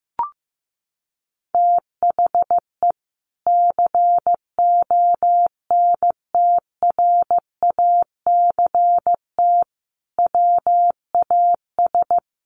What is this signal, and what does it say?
Morse code, heard as a keyed tone, about 15 wpm, 700 Hz. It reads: THE CONTRACT WAS
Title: Sentences from Top 1400 Words 15wpm